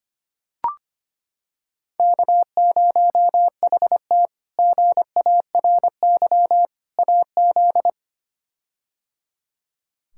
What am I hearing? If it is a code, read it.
Morse code, heard as a keyed tone, about 25 wpm, 700 Hz. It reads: K0HT GARY AZ
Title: K1USN SST QSO Practice - 25wpm